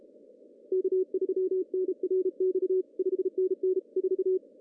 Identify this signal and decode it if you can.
Morse code, heard as a keyed tone, about 34 wpm, 370 Hz. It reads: K3NRX 5NN 4
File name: CW / cqww-cw-k5tr